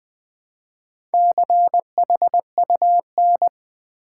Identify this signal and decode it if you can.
Morse code, heard as a keyed tone, about 20 wpm, 700 Hz. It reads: CHUN
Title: Build A Word - 1 Letter At A Time - 5-Letter Words - 20wpm